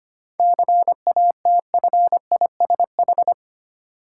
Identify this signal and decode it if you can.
Morse code, heard as a keyed tone, about 25 wpm, 700 Hz. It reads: CATFISH
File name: practice / 25wpm / Difficult Words with Hints - 25wpm